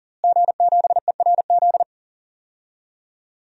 Morse code, heard as a keyed tone, about 40 wpm, 700 Hz. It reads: G7ERZ
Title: DXCC-100 Call Signs - 5 Characters - Repeated 3x - 40wpm